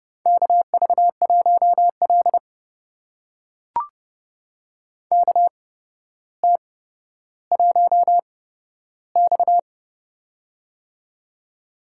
Morse code, heard as a keyed tone, about 31 wpm, spaced as for 13 wpm, 700 Hz. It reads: KV1L KT1X
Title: USA Call Signs - 4 Characters - 8x Character Spacing - 30wpm